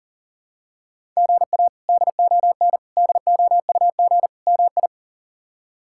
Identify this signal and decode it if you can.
Morse code, heard as a keyed tone, about 37 wpm, 700 Hz. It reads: GA DON DOUG MI
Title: K1USN SST QSO Practice - 40wpm